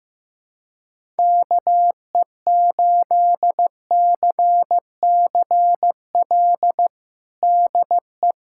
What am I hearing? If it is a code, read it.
Morse code, heard as a keyed tone, about 16 wpm, 700 Hz. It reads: KE8CCL DE
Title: USA Callsign DE Callsign - 6 Characters - 15wpm